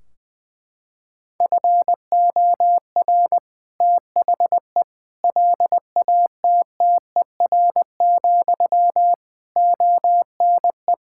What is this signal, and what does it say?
Morse code, heard as a keyed tone, about 20 wpm, 700 Hz. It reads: FOR THE LATTER, ONE
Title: Sentences from Top 1600 Words 20wpm